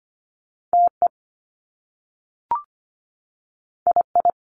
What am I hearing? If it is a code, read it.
Morse code, heard as a keyed tone, about 25 wpm, 700 Hz. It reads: T E I I
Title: Top Two Letter Combinations in Words 25wpm